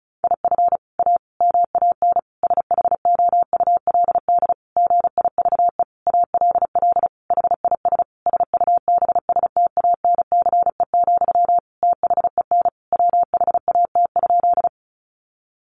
Morse code, heard as a keyed tone, about 34 wpm, 700 Hz. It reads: IF A MAN SHOULD GIVE ALL HIS SUBSTANCE, THEN WHAT?
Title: Sentences from Top 1500 Words 35wpm